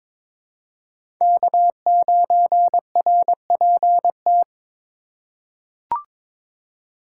Morse code, heard as a keyed tone, about 22 wpm, 700 Hz. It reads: K9RPT
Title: USA Call Signs - 8x Character Spacing - 22wpm